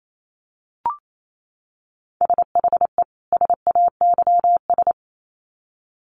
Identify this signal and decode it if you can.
Morse code, heard as a keyed tone, about 28 wpm, 700 Hz. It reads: SHE SAYS
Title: Sets of 2 Words 28wpm